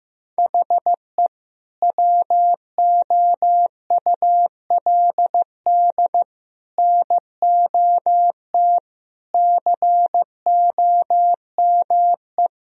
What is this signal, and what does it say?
Morse code, heard as a keyed tone, about 16 wpm, 700 Hz. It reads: HE WOULD NOT COME
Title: Sentences from Top 1000 Words - Review Part 1 - 15wpm